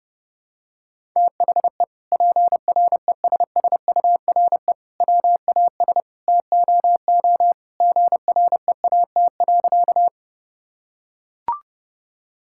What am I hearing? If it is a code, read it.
Morse code, heard as a keyed tone, about 32 wpm, 700 Hz. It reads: THE PRESSURE WAS TOO GREAT.
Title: Sentences from Top 1000 Words - Review Part 6 - 30wpm